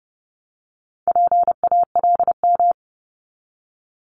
Morse code, heard as a keyed tone, about 30 wpm, 700 Hz. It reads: PALM
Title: Multi-letter Phonemes 30wpm